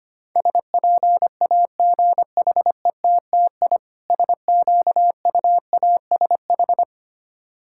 Morse code, heard as a keyed tone, about 25 wpm, 700 Hz. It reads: SPAGHETTI SQUASH
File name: practice / 25wpm / Difficult Words with Hints - 25wpm